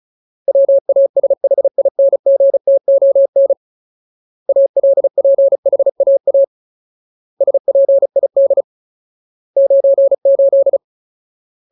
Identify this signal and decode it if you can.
Morse code, heard as a keyed tone, about 34 wpm, 550 Hz. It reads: WASHINGTON ALPHAA SPID 98
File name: Ultimate Word List / 35WPM Ultimate